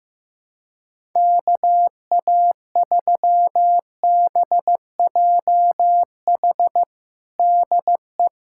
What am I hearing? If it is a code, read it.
Morse code, heard as a keyed tone, about 15 wpm, 700 Hz. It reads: KA3BJH DE
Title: USA Callsign DE Callsign - 6 Characters - 15wpm